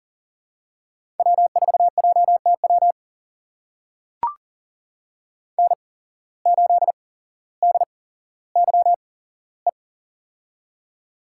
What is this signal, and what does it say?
Morse code, heard as a keyed tone, about 40 wpm, spaced as for 17 wpm, 700 Hz. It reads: W4JTW N8DYE
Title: USA Call Signs - 5 Characters - 8x Character Spacing - 40wpm